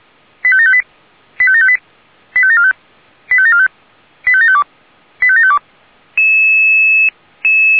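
Five-tone selective call (ZVEI), heard as quick runs of five tones, tones of about 70 ms. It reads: 85758 85758 85745 85745 85752 85752
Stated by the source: ZVEI five-tone selective call received on an SDR.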